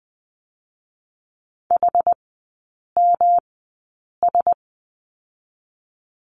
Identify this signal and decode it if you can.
Morse code, heard as a keyed tone, about 19 wpm, 700 Hz. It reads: H M S
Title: 121 Lesson 20e - Number 3 - Three Characters 20wpm